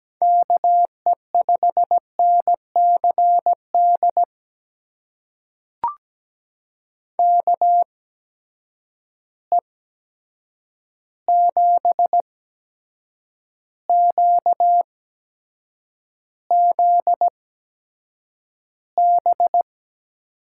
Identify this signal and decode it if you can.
Morse code, heard as a keyed tone, about 17 wpm, spaced as for 7 wpm, 700 Hz. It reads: KE5NCD KE7QZB
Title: USA Call Signs - 6 Characters - 8x Character Spacing - 17wpm